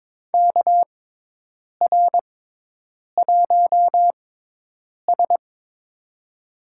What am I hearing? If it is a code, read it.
Morse code, heard as a keyed tone, about 22 wpm, spaced as for 8 wpm, 700 Hz. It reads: KR1S
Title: USA Call Signs - 4 Characters - 6x Character Spacing - 22wpm